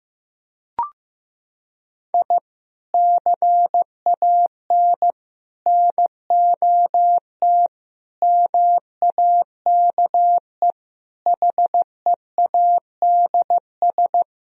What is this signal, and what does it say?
Morse code, heard as a keyed tone, about 15 wpm, 700 Hz. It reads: I CAN NOT MAKE HEADS
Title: Sentences from Top 1000 Words - Review Part 3 - 15wpm